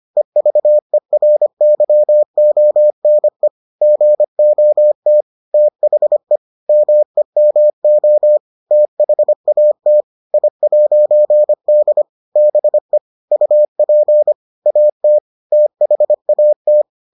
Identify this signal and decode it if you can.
Morse code, heard as a keyed tone, about 25 wpm, 600 Hz. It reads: EVERYONE GOT THE MEMO THAT I'D BE UP AT THAT